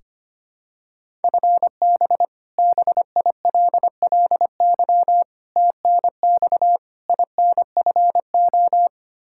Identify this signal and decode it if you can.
Morse code, heard as a keyed tone, about 25 wpm, 700 Hz. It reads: FB BILLY TNX INFO
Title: K1USN SST QSO Practice 25wpm